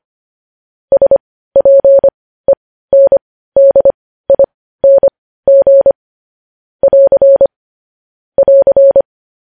Morse code, heard as a keyed tone, about 25 wpm, spaced as for 15 wpm, 550 Hz. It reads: SPENDING <AR> <AR>
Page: news-headlines-in-morse-code-at-15-wpm